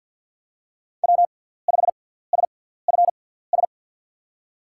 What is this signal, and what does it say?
Morse code, heard as a keyed tone, about 47 wpm, spaced as for 17 wpm, 700 Hz. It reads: W5SFS
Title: USA Call Signs - 5 Characters - 6x Character Spacing - 50wpm